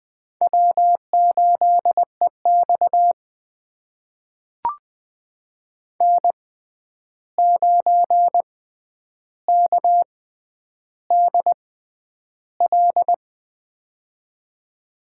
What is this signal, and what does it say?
Morse code, heard as a keyed tone, about 20 wpm, spaced as for 10 wpm, 700 Hz. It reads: W8EX N9KDL
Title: USA Call Signs - 6x Character Spacing - 20wpm